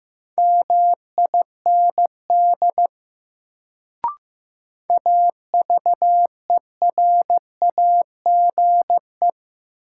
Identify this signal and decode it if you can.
Morse code, heard as a keyed tone, about 15 wpm, 700 Hz. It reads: MIND AVERAGE
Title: Sets of 5 Words - New Word Drill - 15wpm